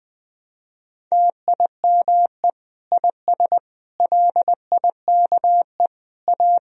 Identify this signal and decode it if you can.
Morse code, heard as a keyed tone, about 20 wpm, 700 Hz. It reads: TIME IS LIKE A
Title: Sentences from Top 1000 Words - Review Part 3 - 20wpm